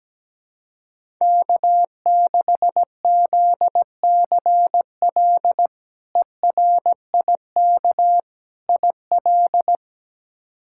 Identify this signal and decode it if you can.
Morse code, heard as a keyed tone, about 17 wpm, 700 Hz. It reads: K6ZCL ERIK IL
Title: K1USN SST QSO Practice - 17wpm